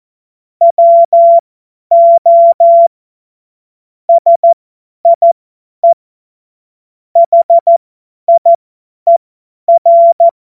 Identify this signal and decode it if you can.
Morse code, heard as a keyed tone, about 14 wpm, spaced as for 10 wpm, 680 Hz. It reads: WO SIE HIER
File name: Morsekurs / Witze / Witz 11. Juli 2023 50 F70